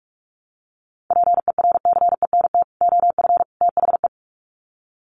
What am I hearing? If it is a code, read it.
Morse code, heard as a keyed tone, about 44 wpm, 700 Hz. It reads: PERCENT OF THE
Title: Sets of 3 Words 45wpm